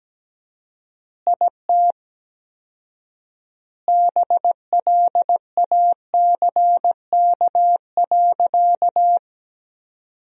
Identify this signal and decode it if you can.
Morse code, heard as a keyed tone, about 17 wpm, 700 Hz. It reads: IT BLACK.
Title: 4x Word-Spacing - Sentences from Top 400 Words - 17wpm